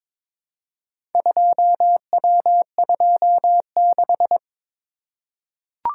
Morse code, heard as a keyed tone, about 22 wpm, 700 Hz. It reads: 2W26
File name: practice / 22wpm / Four Letter-Number Combinations - 22wpm